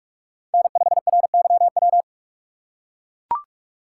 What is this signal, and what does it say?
Morse code, heard as a keyed tone, about 42 wpm, 700 Hz. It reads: N5RYW
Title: USA Call Signs - 5 Characters - 4x Character Spacing - 45wpm